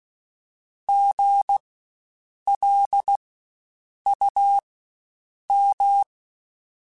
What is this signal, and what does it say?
Morse code, heard as a keyed tone, about 16 wpm, spaced as for 7 wpm, 800 Hz. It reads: GLUM
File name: Spaced Alphas / Spaced Alphas mp3s / Gs sp @15